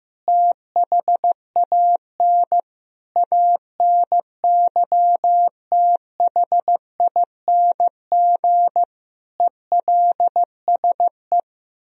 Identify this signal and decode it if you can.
Morse code, heard as a keyed tone, about 15 wpm, 700 Hz. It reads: THAN ANYTHING ELSE
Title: Sentences from Top 1000 Words - Review Part 4 - 15wpm